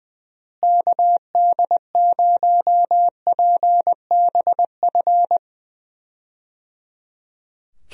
Morse code, heard as a keyed tone, about 20 wpm, 700 Hz. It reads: KD0PBF
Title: USA Call Signs - 6 Characters - Repeated 3x - 20wpm